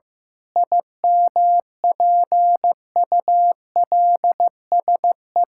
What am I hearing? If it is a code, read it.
Morse code, heard as a keyed tone, about 15 wpm, 700 Hz. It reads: IMPULSE
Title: Pangram Challenge v2 - 15wpm